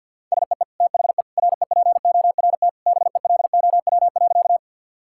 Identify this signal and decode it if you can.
Morse code, heard as a keyed tone, about 50 wpm, 700 Hz. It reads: SEE THE REPORT BELOW.
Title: Sentences from Top 1000 Words - Review Part 6 - 50wpm